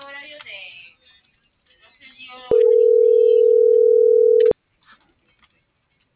Ringback tone (North America), 440 + 480 Hz.